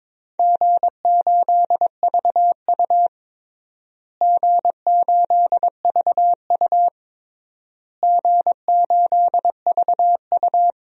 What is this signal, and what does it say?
Morse code, heard as a keyed tone, about 21 wpm, 700 Hz. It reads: G8VU G8VU G8VU
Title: DXCC-100 Call Signs - 4 Characters - Repeated 3x - 22wpm